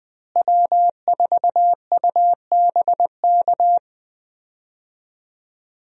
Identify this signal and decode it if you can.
Morse code, heard as a keyed tone, about 20 wpm, 700 Hz. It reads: W4UBK
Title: USA Call Signs - 5 Characters - Repeated 3x - 20wpm